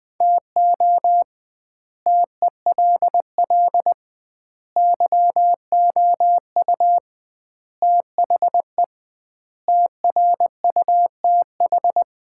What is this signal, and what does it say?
Morse code, heard as a keyed tone, about 20 wpm, 700 Hz. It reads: TO TELL YOU THE TRUTH
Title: 2x Word-Spacing - Sets of 5 Words - 20wpm